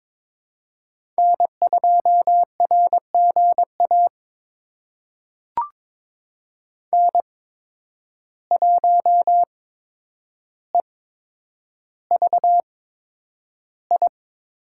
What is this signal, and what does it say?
Morse code, heard as a keyed tone, about 22 wpm, spaced as for 9 wpm, 700 Hz. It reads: N2RGA N1EVI
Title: USA Call Signs - 5 Characters - 8x Character Spacing - 22wpm